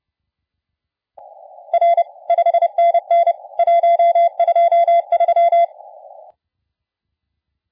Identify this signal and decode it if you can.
Morse code, heard as a keyed tone, about 29 wpm, 680 Hz. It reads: R 5NN 123
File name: la4c / sac-cw-2017 / CONTEST AUDIO 1100Z